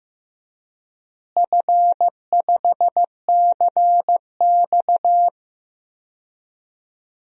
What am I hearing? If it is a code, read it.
Morse code, heard as a keyed tone, about 15 wpm, 700 Hz. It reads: F5CX
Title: DXCC-100 Call Signs - 4 Characters - Repeated 2x - 15wpm